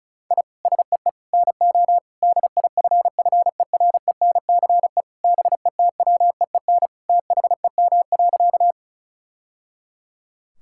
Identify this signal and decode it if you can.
Morse code, heard as a keyed tone, about 36 wpm, 700 Hz. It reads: I SEE NO DIFFERENCE BETWEEN THEM.
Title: Sentences from Top 1000 Words - Review Part 5 - 35wpm